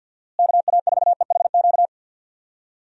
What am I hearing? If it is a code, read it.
Morse code, heard as a keyed tone, about 50 wpm, 700 Hz. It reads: KA4EHQ
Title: USA Call Signs - 6 Characters - Repeated 3x - 50wpm